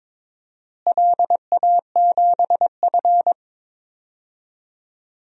Morse code, heard as a keyed tone, about 22 wpm, 700 Hz. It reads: LA7F
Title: DXCC-100 Call Signs - 4 Characters - Repeated 3x - 22wpm